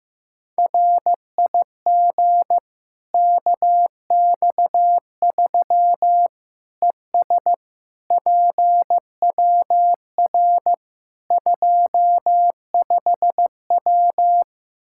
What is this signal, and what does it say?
Morse code, heard as a keyed tone, about 16 wpm, 700 Hz. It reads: RIG KX3 ES PWR 25W
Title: Practice QSO - Exchange 2 - Info - 15wpm